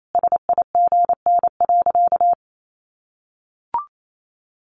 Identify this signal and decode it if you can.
Morse code, heard as a keyed tone, about 28 wpm, 700 Hz. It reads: SIGN.
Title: Sentences from Top 1100 Words 28wpm